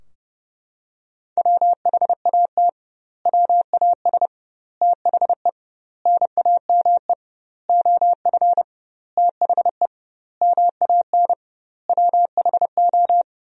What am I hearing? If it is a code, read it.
Morse code, heard as a keyed tone, about 30 wpm, 700 Hz. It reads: WHAT WAS THE NAME OF THE MAN WHO
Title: 2x Word-Spacing - Sentences from Top 300 Words - 30wpm